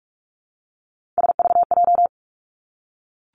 Morse code, heard as a keyed tone, about 46 wpm, 700 Hz. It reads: SVJ
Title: Three Challenging Letter-Number Combinations 45wpm